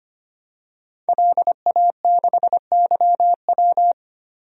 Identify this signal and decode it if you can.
Morse code, heard as a keyed tone, about 25 wpm, 700 Hz. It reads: LA6YW
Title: DXCC-100 Call Signs - 5 Characters - 25wpm